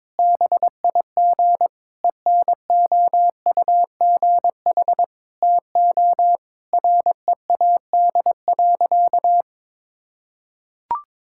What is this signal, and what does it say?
Morse code, heard as a keyed tone, about 22 wpm, 700 Hz. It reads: BIG ENOUGH TO READ.
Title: Sentences from Top 1000 Words - Review Part 5 - 22wpm